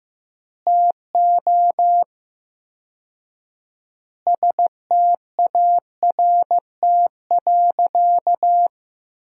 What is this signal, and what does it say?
Morse code, heard as a keyed tone, about 15 wpm, 700 Hz. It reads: TO START.
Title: 4x Word-Spacing - Sentences from Top 300 Words - 15wpm